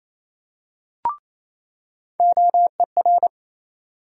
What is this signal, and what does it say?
Morse code, heard as a keyed tone, about 27 wpm, 700 Hz. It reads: OER
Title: USA Call Sign Postfixes - 3 Letters 28wpm